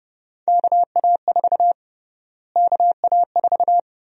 Morse code, heard as a keyed tone, about 32 wpm, 700 Hz. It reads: KA4 KA4
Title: USA Call Sign Prefixes - Repeated 2x - 30wpm